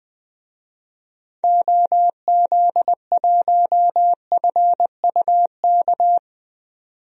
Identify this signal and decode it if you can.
Morse code, heard as a keyed tone, about 20 wpm, 700 Hz. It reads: OZ1FUK
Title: DXCC-100 Call Signs - 6 Characters - 20wpm